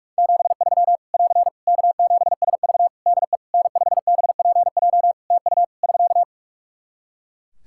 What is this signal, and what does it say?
Morse code, heard as a keyed tone, about 46 wpm, 700 Hz. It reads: 73 <AR> K7SV DE N5BPJ TU <SK>
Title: Practice QSO - Exchange 4 - Ending - 45wpm